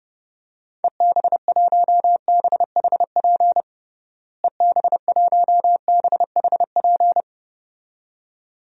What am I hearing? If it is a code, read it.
Morse code, heard as a keyed tone, about 30 wpm, 700 Hz. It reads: EB1BHP EB1BHP
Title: DXCC-100 Call Signs - 6 Characters - Repeated 2x - 30wpm